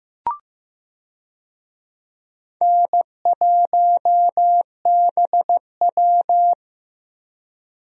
Morse code, heard as a keyed tone, about 15 wpm, 700 Hz. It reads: N1BW
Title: USA Call Signs - 4 Characters - 15wpm